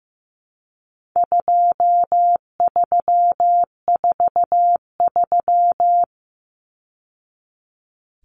Morse code, heard as a keyed tone, about 15 wpm, 700 Hz. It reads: 2343
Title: Numbers - 4-Digit 15wpm